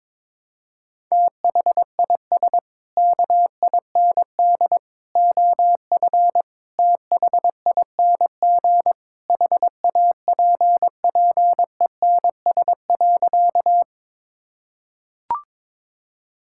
Morse code, heard as a keyed tone, about 21 wpm, 700 Hz. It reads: THIS KIND OF THING HAPPENS.
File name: practice / 22wpm / Sentences with 3-Word Phrases - 22wpm